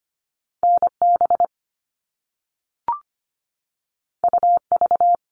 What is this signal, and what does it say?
Morse code, heard as a keyed tone, about 25 wpm, 700 Hz. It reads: N B U V
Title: All Cut Numbers - 2-Digit 25wpm